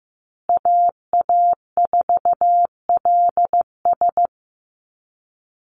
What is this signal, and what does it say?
Morse code, heard as a keyed tone, about 15 wpm, 700 Hz. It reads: AA4LS
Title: Call Signs Repeated 2x - 15wpm